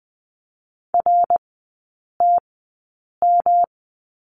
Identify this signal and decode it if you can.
Morse code, heard as a keyed tone, about 20 wpm, spaced as for 8 wpm, 690 Hz. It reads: RTM
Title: 159 Lesson 26e - Letter g - Three Characters 20wpm